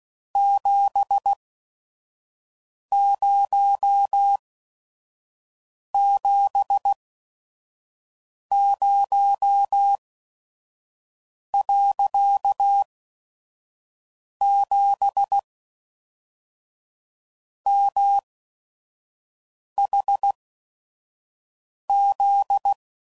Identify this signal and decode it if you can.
Morse code, heard as a keyed tone, about 16 wpm, spaced as for 5 wpm, 800 Hz. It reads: 7070.7 MHZ
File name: CW Code Groups @5WPM / QSOs / kc7sac